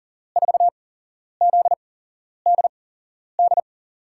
Morse code, heard as a keyed tone, about 40 wpm, spaced as for 11 wpm, 700 Hz. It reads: 4ZDD
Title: USA Call Signs - 8x Character Spacing - 40wpm